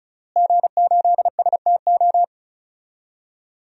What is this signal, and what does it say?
Morse code, heard as a keyed tone, about 36 wpm, 700 Hz. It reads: G8STO
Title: DXCC-100 Call Signs - 5 Characters - Repeated 3x - 35wpm